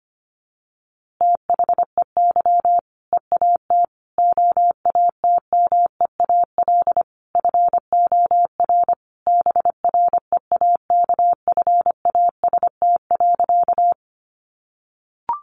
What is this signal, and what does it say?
Morse code, heard as a keyed tone, about 25 wpm, 700 Hz. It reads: THEY EAT OATMEAL FOR BREAKFAST.
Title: Easy Phrases 25wpm